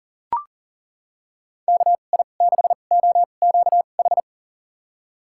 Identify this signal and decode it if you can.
Morse code, heard as a keyed tone, about 40 wpm, spaced as for 29 wpm, 700 Hz. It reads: KI6OQH
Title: USA Call Signs - 6 Characters - 2x Character Spacing - 40wpm